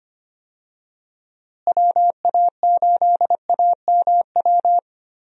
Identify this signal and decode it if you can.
Morse code, heard as a keyed tone, about 25 wpm, 700 Hz. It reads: WA8AMW
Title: USA Call Signs - 6 Characters - Repeated 3x - 25wpm